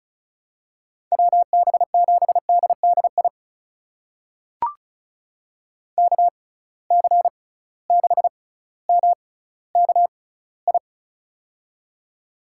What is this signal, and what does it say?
Morse code, heard as a keyed tone, about 34 wpm, spaced as for 18 wpm, 700 Hz. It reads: WB7DDI KC6MKI
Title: USA Call Signs - 6 Characters - 6x Character Spacing - 35wpm